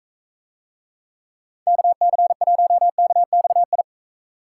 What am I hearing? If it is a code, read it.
Morse code, heard as a keyed tone, about 45 wpm, 700 Hz. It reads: KC1KXI